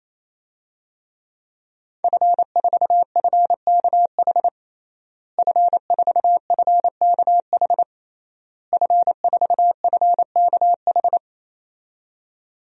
Morse code, heard as a keyed tone, about 28 wpm, 700 Hz. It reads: F4FKH F4FKH F4FKH
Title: DXCC-100 Call Signs - 5 Characters - Repeated 3x - 28wpm